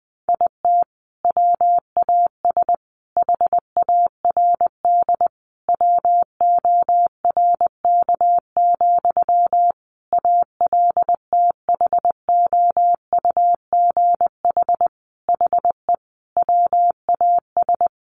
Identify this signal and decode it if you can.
Morse code, heard as a keyed tone, about 20 wpm, 700 Hz. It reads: IT WAS HARD WORK, ALTHOUGH HE WAS
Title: Sentences from Top 600 Words 20wpm